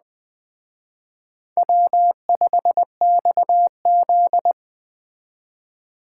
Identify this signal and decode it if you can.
Morse code, heard as a keyed tone, about 21 wpm, 700 Hz. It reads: W5XZ
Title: Speed Racing - USA 1x2 Call Signs 17wpm